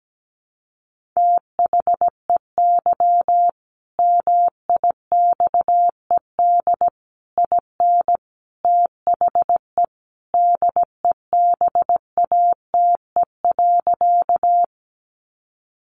Morse code, heard as a keyed tone, about 17 wpm, 700 Hz. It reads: THEY MIXED IN THE DEBATE.
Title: Sentences from Top 1500 Words 17wpm